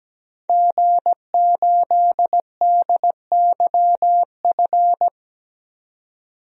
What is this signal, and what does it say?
Morse code, heard as a keyed tone, about 17 wpm, 700 Hz. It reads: G8DYF
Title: DXCC-100 Call Signs - 5 Characters - 17wpm